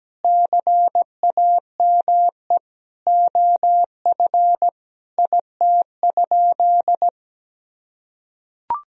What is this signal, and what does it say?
Morse code, heard as a keyed tone, about 17 wpm, 700 Hz. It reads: CAME OF IT?
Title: Sentences from Top 1000 Words - Review Part 3 - 17wpm